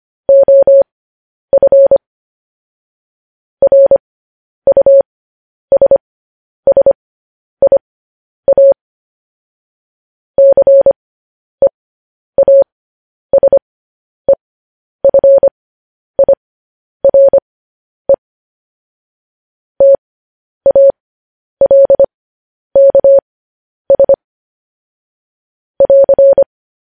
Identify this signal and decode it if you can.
Morse code, heard as a keyed tone, about 25 wpm, spaced as for 10 wpm, 550 Hz. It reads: OF RUSSIA CEASEFIRE TALKS <AR>
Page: news-headlines-in-morse-code-at-10-wpm